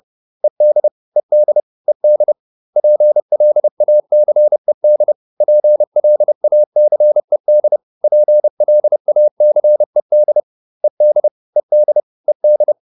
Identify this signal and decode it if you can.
Morse code, heard as a keyed tone, about 30 wpm, 600 Hz. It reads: ED ED ED PLACED PLACED PLACED ED ED ED